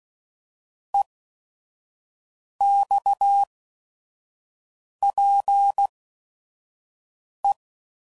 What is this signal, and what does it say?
Morse code, heard as a keyed tone, about 16 wpm, spaced as for 5 wpm, 800 Hz. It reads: EXPE
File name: CW Code Groups @5WPM / QSOs / QSO WA7QOX 1